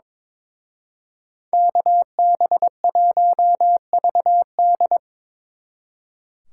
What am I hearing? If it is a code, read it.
Morse code, heard as a keyed tone, about 26 wpm, 700 Hz. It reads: KB1VD
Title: Speed Racing - USA 2x2 Call Signs 22wpm